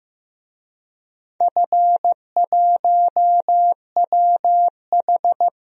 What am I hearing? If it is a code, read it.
Morse code, heard as a keyed tone, about 15 wpm, 700 Hz. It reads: F1WH
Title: DXCC-100 Call Signs - 4 Characters - Repeated 3x - 15wpm